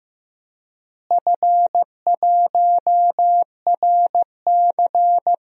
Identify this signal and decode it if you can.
Morse code, heard as a keyed tone, about 15 wpm, 700 Hz. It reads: F1RC
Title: DXCC-100 Call Signs - 4 Characters - Repeated 3x - 15wpm